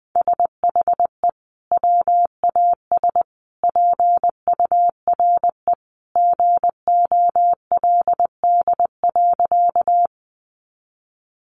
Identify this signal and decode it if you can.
Morse code, heard as a keyed tone, about 20 wpm, 700 Hz. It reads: SHE WAS PURE GOLD.
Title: Sentences from Top 1500 Words 20wpm